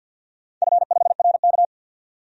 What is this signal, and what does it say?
Morse code, heard as a keyed tone, about 49 wpm, 700 Hz. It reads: F5RK
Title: DXCC-100 Call Signs - 4 Characters - 50wpm